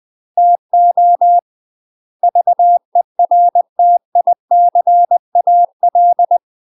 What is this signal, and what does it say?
Morse code, heard as a keyed tone, about 20 wpm, 700 Hz. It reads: TO VERTICAL
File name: UK MASTER QSOs TRAINING / 20WPM / UK MASTER QSOs 0030